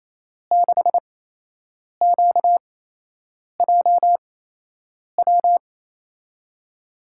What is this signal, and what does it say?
Morse code, heard as a keyed tone, about 28 wpm, spaced as for 8 wpm, 700 Hz. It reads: 6QJW
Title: USA Call Signs - 6 Characters - 8x Character Spacing - 28wpm